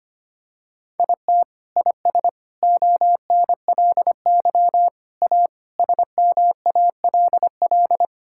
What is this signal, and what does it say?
Morse code, heard as a keyed tone, about 25 wpm, 700 Hz. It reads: IT IS ONLY A SMALL
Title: Sentences from Top 1000 Words - Review Part 5 - 25wpm